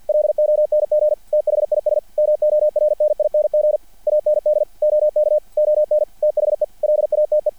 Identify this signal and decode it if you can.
Morse code, heard as a keyed tone, about 50 wpm, 600 Hz, with radio noise. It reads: LONG THIS MORNING AND OK ON THE LATE
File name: archived / CRSnet